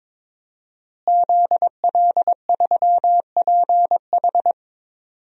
Morse code, heard as a keyed tone, about 22 wpm, 700 Hz. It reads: ZL3PH
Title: DXCC-100 Call Signs - 5 Characters - Repeated 2x - 22wpm